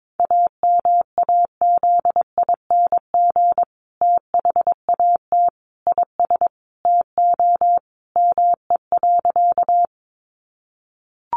Morse code, heard as a keyed tone, about 22 wpm, 700 Hz. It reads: AMAZING THAT IS TO ME.
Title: Sentences from Top 1500 Words 22wpm